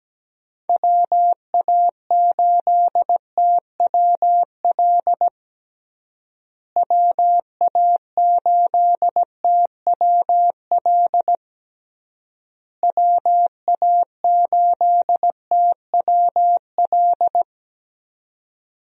Morse code, heard as a keyed tone, about 17 wpm, 700 Hz. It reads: WA8TWL WA8TWL WA8TWL
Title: USA Call Signs - 6 Characters - Repeated 3x - 17wpm